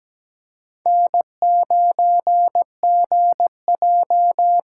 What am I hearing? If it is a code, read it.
Morse code, heard as a keyed tone, about 17 wpm, 700 Hz. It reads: N9GJ
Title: USA Call Signs - 5 Characters - 6x Character Spacing - 17wpm